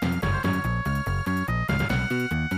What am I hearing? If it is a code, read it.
There is no code.